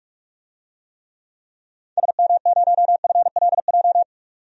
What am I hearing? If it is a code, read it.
Morse code, heard as a keyed tone, about 44 wpm, 700 Hz. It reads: SM0FLJ
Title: DXCC-100 Call Signs - 6 Characters - 45wpm